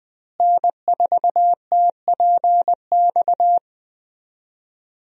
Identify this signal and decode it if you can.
Morse code, heard as a keyed tone, about 21 wpm, 700 Hz. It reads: N4TPX
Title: USA Call Signs - 5 Characters - Repeated 3x - 20wpm